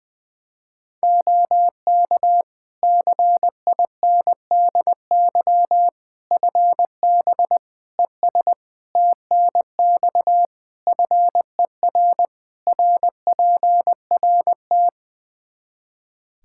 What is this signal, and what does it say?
Morse code, heard as a keyed tone, about 20 wpm, 700 Hz. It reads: OK CINDY FB ES TNX FER RPRT
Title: Practice QSO - Exchange 2 - Info - 20wpm